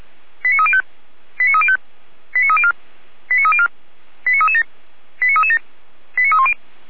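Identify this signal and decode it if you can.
Five-tone selective call (ZVEI), heard as quick runs of five tones, tones of about 80 ms. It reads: 89305 89305 89304 89304 89337 89337 89310
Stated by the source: ZVEI five-tone selective call received on SDR.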